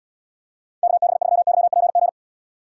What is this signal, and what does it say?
Morse code, heard as a keyed tone, about 77 wpm, 700 Hz. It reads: KB3YPG